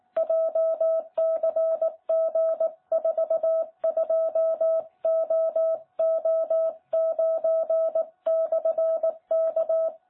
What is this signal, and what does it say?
Morse code, heard as a keyed tone, about 18 wpm, 660 Hz. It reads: JCG42OO9/K
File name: jh8cbh / 2021cw